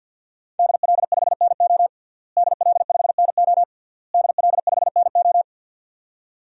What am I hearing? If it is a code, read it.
Morse code, heard as a keyed tone, about 53 wpm, 700 Hz. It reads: DL5NO DL5NO DL5NO
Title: DXCC-100 Call Signs - 5 Characters - Repeated 3x - 50wpm